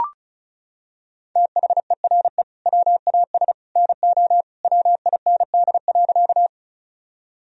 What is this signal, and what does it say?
Morse code, heard as a keyed tone, about 36 wpm, 700 Hz. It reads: THERE WAS NO WIND.
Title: Sentences from Top 1000 Words - Review Part 2 - 35wpm